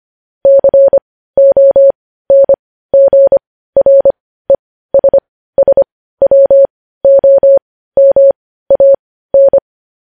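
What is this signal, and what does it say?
Morse code, heard as a keyed tone, about 25 wpm, spaced as for 15 wpm, 550 Hz. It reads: CONGRESSWOMAN
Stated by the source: Morse news headlines practice at 15 wpm.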